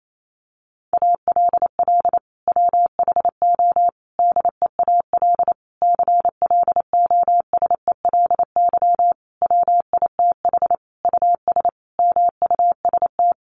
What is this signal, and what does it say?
Morse code, heard as a keyed tone, about 28 wpm, 700 Hz. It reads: ALL WHO DEAL CLOSELY WITH US MUST
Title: Sentences from Top 1400 Words 28wpm